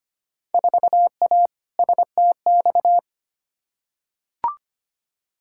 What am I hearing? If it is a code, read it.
Morse code, heard as a keyed tone, about 25 wpm, 700 Hz. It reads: 4A STX
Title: Field Day Exchanges Encore - 25wpm